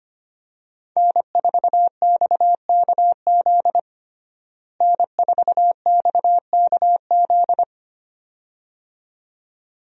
Morse code, heard as a keyed tone, about 25 wpm, 700 Hz. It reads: N4XKZ N4XKZ
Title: USA Call Signs - 5 Characters - Repeated 2x - 25wpm